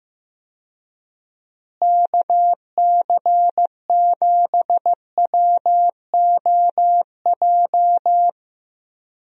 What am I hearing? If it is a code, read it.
Morse code, heard as a keyed tone, about 15 wpm, 700 Hz. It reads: KC7WOJ
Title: USA Call Signs - 6 Characters - Repeated 3x - 15wpm